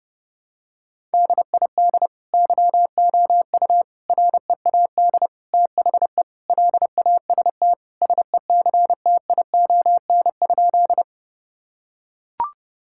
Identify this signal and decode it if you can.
Morse code, heard as a keyed tone, about 30 wpm, 700 Hz. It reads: DID YOU READ THE LAST SECTION?
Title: Sentences from Top 1000 Words - Review Part 5 - 30wpm